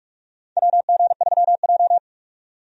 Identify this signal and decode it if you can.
Morse code, heard as a keyed tone, about 46 wpm, 700 Hz. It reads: WG3J
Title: USA Call Signs - 4 Characters - 45wpm